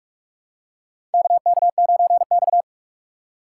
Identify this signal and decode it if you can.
Morse code, heard as a keyed tone, about 46 wpm, 700 Hz. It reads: KK9X
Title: USA Call Signs - 4 Characters - 2x Character Spacing - 45wpm